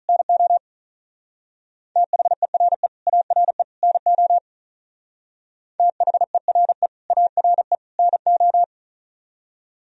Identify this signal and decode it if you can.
Morse code, heard as a keyed tone, about 44 wpm, 700 Hz. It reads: NO THERE ARE NO THERE ARE NO